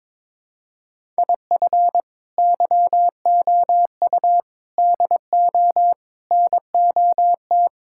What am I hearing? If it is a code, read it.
Morse code, heard as a keyed tone, about 22 wpm, 700 Hz. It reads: IF YOU DO NOT
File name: practice / 22wpm / Build A Phrase - 5-Word Phrases - 22wpm